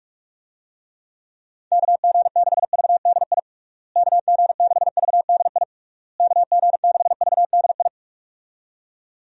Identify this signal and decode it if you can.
Morse code, heard as a keyed tone, about 41 wpm, 700 Hz. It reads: KG6VDI KG6VDI KG6VDI
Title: USA Call Signs - 6 Characters - Repeated 3x - 45wpm